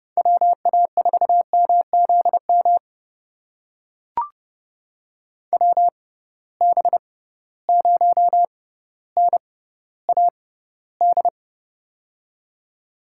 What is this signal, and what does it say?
Morse code, heard as a keyed tone, about 30 wpm, spaced as for 15 wpm, 700 Hz. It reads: WA4MZM WB0NAD
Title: USA Call Signs - 6 Characters - 6x Character Spacing - 30wpm